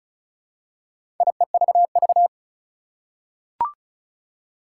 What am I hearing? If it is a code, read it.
Morse code, heard as a keyed tone, about 36 wpm, 700 Hz. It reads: IEVV
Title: Four Challenging Letter-Number Combinations 35wpm